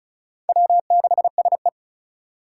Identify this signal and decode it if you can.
Morse code, heard as a keyed tone, about 36 wpm, 700 Hz. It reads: W6SE
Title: USA Call Signs - 4 Characters - 4x Character Spacing - 35wpm